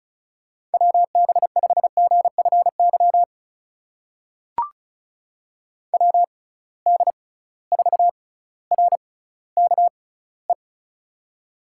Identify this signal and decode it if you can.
Morse code, heard as a keyed tone, about 36 wpm, spaced as for 18 wpm, 700 Hz. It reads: WB5GFY WD4RKE
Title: USA Call Signs - 6 Characters - 6x Character Spacing - 35wpm